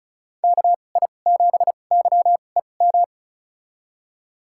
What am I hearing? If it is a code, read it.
Morse code, heard as a keyed tone, about 34 wpm, spaced as for 25 wpm, 700 Hz. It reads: KI7YEM
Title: USA Call Signs - 6 Characters - 2x Character Spacing - 35wpm